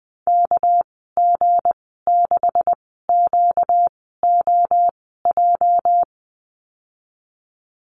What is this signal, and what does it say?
Morse code, heard as a keyed tone, about 20 wpm, spaced as for 14 wpm, 700 Hz. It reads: KG6QOJ
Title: 238 Lesson 38g - Letter q - Call Signs 20wpm